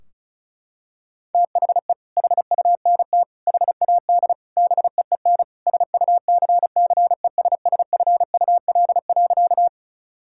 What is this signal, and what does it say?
Morse code, heard as a keyed tone, about 36 wpm, 700 Hz. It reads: THE HUNT HAD BEEN SUCCESSFUL.
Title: Sentences from Top 1600 Words 35wpm